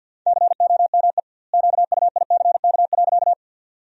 Morse code, heard as a keyed tone, about 51 wpm, 700 Hz. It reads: COME QUICK.
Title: 2x Word-Spacing - Sentences from Top 500 Words - 50wpm